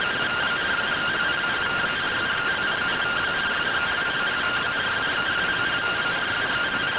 Two-tone teleprinter signal, about 170 Hz shift.